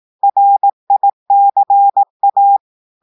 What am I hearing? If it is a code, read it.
Morse code, heard as a keyed tone, about 18 wpm, 800 Hz. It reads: RICA
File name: audios / portugais / portugais texte en clair 1 vitesse 18 mots minutes